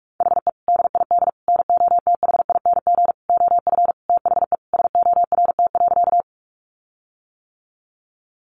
Morse code, heard as a keyed tone, about 44 wpm, 700 Hz. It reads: HE DID NOTHING OF THE SORT.
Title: Sentences from Top 1300 Words 45wpm